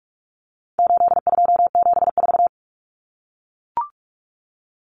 Morse code, heard as a keyed tone, about 44 wpm, 700 Hz. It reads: 8274
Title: Numbers - 4-Digit 45wpm